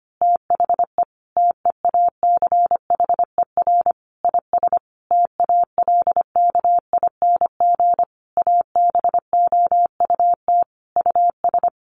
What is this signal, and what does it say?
Morse code, heard as a keyed tone, about 25 wpm, 700 Hz. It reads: THE TEACHER IS TALKING ABOUT US
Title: Easy Phrases 25wpm